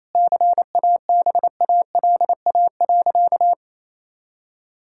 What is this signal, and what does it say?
Morse code, heard as a keyed tone, about 28 wpm, 700 Hz. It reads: CABALA.
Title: Pangram Challenge v2 - 28wpm